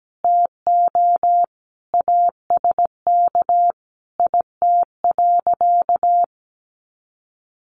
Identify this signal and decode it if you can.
Morse code, heard as a keyed tone, about 17 wpm, 700 Hz. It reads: TO ASK IT.
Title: Sentences from Top 1300 Words 17wpm